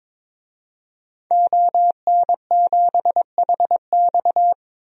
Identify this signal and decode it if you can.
Morse code, heard as a keyed tone, about 21 wpm, 700 Hz. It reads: ON7HX
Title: DXCC-100 Call Signs - 5 Characters - 22wpm